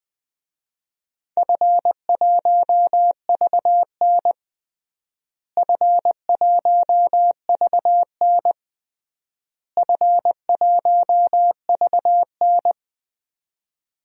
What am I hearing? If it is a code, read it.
Morse code, heard as a keyed tone, about 20 wpm, 700 Hz. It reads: F1VN F1VN F1VN
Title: DXCC-100 Call Signs - 4 Characters - Repeated 3x - 20wpm